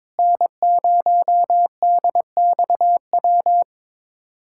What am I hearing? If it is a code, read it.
Morse code, heard as a keyed tone, about 22 wpm, 700 Hz. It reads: N0DXW
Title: USA Callsign DE Callsign - 5 Characters - 22wpm